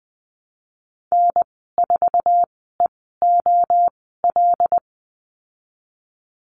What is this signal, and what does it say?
Morse code, heard as a keyed tone, about 19 wpm, spaced as for 14 wpm, 700 Hz. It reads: N4EOL
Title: 088 Lesson 15g - Number 5 - Call Signs 20wpm